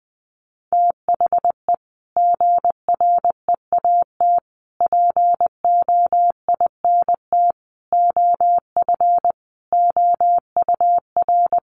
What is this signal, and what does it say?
Morse code, heard as a keyed tone, about 20 wpm, 700 Hz. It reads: THE GREAT POINT OF OUR
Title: Sentences from Top 1500 Words 20wpm